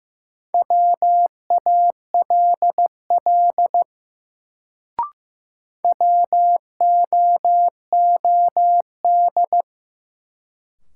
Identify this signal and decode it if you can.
Morse code, heard as a keyed tone, about 15 wpm, 700 Hz. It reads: WALL WOOD
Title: Sentences from Top 1100 Words - New Word Drill Alt - 15wpm